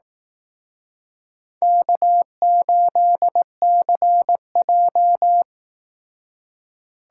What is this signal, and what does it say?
Morse code, heard as a keyed tone, about 18 wpm, 700 Hz. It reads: K8CJ
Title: Speed Racing - USA Call Signs 15wpm